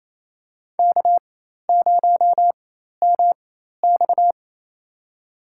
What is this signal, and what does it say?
Morse code, heard as a keyed tone, about 28 wpm, spaced as for 13 wpm, 700 Hz. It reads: K0MX
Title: USA Call Signs - 4 Characters - 4x Character Spacing - 28wpm